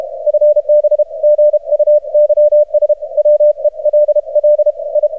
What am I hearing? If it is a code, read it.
Morse code, heard as a keyed tone, about 51 wpm, spaced as for 36 wpm, 600 Hz. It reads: FB GUYS WELL I
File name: archived / CRSnet